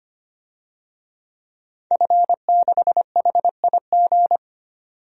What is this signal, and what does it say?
Morse code, heard as a keyed tone, about 25 wpm, 700 Hz. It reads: F6HIG
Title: DXCC-100 Call Signs - 5 Characters - 25wpm